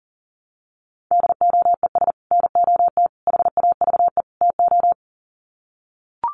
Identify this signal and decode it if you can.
Morse code, heard as a keyed tone, about 40 wpm, 700 Hz. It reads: DOES NOT HAVE TO
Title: Sets of 4 Words 40wpm